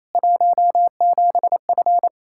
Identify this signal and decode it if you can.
Morse code, heard as a keyed tone, about 28 wpm, 700 Hz. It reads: 17F
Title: Practice QSO - Exchange 2 - Info - 28wpm